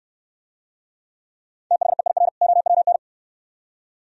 Morse code, heard as a keyed tone, about 69 wpm, 700 Hz. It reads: THEIR OWN